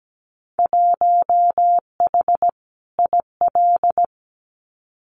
Morse code, heard as a keyed tone, about 17 wpm, 700 Hz. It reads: 1H IL
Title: Winter Field Day Exchanges 17wpm